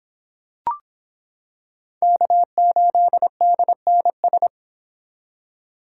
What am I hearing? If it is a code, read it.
Morse code, heard as a keyed tone, about 25 wpm, 700 Hz. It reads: K8DNS